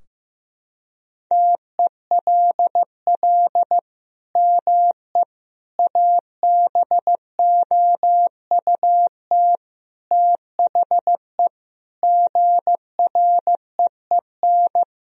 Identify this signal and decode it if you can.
Morse code, heard as a keyed tone, about 16 wpm, 700 Hz. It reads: TELL ME ABOUT THE GREEN
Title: Sentences from Top 1000 Words - Review Part 2 - 15wpm